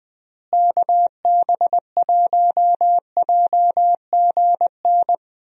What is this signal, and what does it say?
Morse code, heard as a keyed tone, about 20 wpm, 700 Hz. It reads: KB1JGN
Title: USA Call Signs - 6 Characters - Repeated 3x - 20wpm